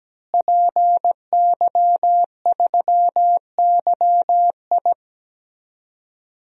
Morse code, heard as a keyed tone, about 17 wpm, 700 Hz. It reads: PY3YI
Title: DXCC-100 Call Signs - 5 Characters - Repeated 3x - 17wpm